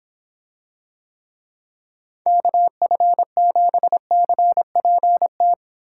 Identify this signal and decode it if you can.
Morse code, heard as a keyed tone, about 25 wpm, 700 Hz. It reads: KF7CPT